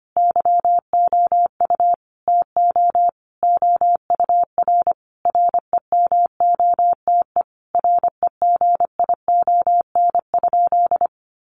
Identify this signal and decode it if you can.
Morse code, heard as a keyed tone, about 25 wpm, 700 Hz. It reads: YOU TO OUR REMOTE REGION?
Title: Sentences from Top 1500 Words 25wpm